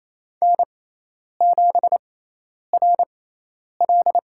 Morse code, heard as a keyed tone, about 27 wpm, spaced as for 10 wpm, 700 Hz. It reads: N7RL
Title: USA Call Signs - 6x Character Spacing - 28wpm